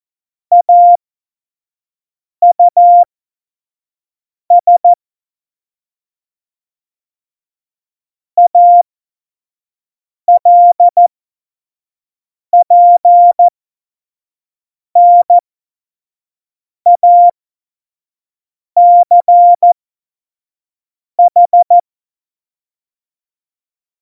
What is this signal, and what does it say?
Morse code, heard as a keyed tone, about 14 wpm, spaced as for 5 wpm, 700 Hz. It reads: AUS ALPNACH
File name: Morsekurs / Witze / Witz 28 Juli 2020, 25 F70